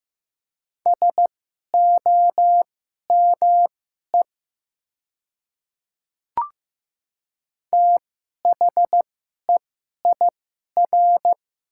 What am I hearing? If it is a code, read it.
Morse code, heard as a keyed tone, about 15 wpm, spaced as for 11 wpm, 700 Hz. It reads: SOME THEIR
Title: 2x Character Spacing - Top 200 Words - 15wpm